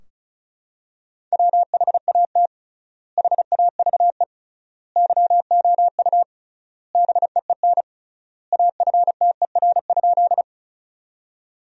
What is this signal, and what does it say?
Morse code, heard as a keyed tone, about 36 wpm, 700 Hz. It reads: WHAT HAVE YOU BEEN AFTER?
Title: 3x Word-Spacing - Sentences from Top 200 Words - 35wpm